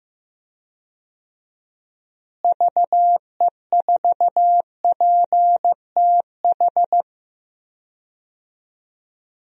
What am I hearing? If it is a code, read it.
Morse code, heard as a keyed tone, about 15 wpm, 700 Hz. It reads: VE4PTH
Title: DXCC-100 Call Signs - 6 Characters - Repeated 2x - 15wpm